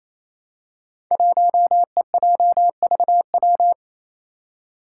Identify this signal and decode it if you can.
Morse code, heard as a keyed tone, about 28 wpm, 700 Hz. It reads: 1EJVW
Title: Five Challenging Letter-Number Combinations 28wpm